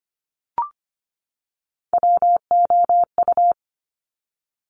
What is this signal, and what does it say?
Morse code, heard as a keyed tone, about 25 wpm, 700 Hz. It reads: WOU
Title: Top Three Letter Combinations in Words 25wpm